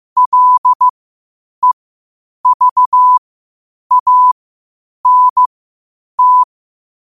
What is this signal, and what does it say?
Morse code, heard as a keyed tone, about 15 wpm, spaced as for 8 wpm, 1000 Hz. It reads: LEVANT
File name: qotdfiles / 8wpm